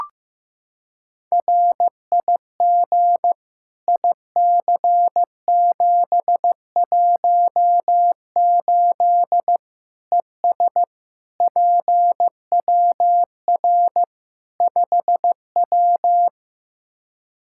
Morse code, heard as a keyed tone, about 15 wpm, 700 Hz. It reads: RIG IC718 ES PWR 5W
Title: K1USN SST QSO Practice 15wpm